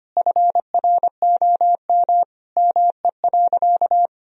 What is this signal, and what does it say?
Morse code, heard as a keyed tone, about 25 wpm, 700 Hz. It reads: FROM ME.
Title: Sentences from Top 1000 Words - Review Part 2 - 25wpm